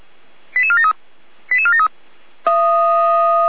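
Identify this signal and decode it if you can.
Five-tone selective call (ZVEI), heard as quick runs of five tones, tones of about 70 ms. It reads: 88473 88473
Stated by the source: ZVEI five-tone call received on SDR.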